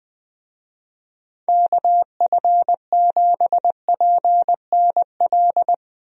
Morse code, heard as a keyed tone, about 20 wpm, 700 Hz. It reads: KF7PNL
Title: USA Call Signs - 6 Characters - Repeated 2x - 20wpm